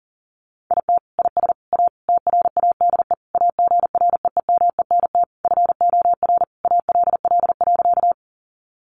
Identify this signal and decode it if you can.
Morse code, heard as a keyed tone, about 38 wpm, 700 Hz. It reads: IT IS A TRADE AGREEMENT FOR ALL.
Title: Sentences from Top 1500 Words 40wpm